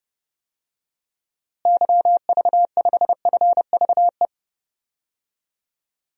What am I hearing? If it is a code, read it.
Morse code, heard as a keyed tone, about 29 wpm, 700 Hz. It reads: YV5FVE
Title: DXCC-100 Call Signs - 6 Characters - Repeated 3x - 30wpm